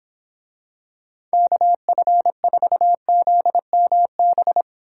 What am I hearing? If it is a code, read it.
Morse code, heard as a keyed tone, about 26 wpm, 700 Hz. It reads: KF4ZMB